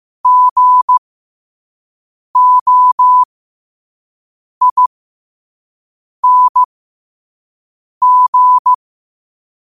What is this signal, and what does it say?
Morse code, heard as a keyed tone, about 15 wpm, spaced as for 5 wpm, 1000 Hz. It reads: GOING
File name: qotdfiles / 5wpm